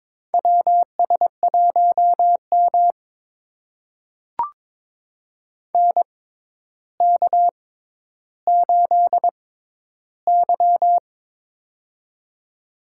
Morse code, heard as a keyed tone, about 22 wpm, spaced as for 11 wpm, 700 Hz. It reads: WS1M NK8Y
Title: USA Call Signs - 4 Characters - 6x Character Spacing - 22wpm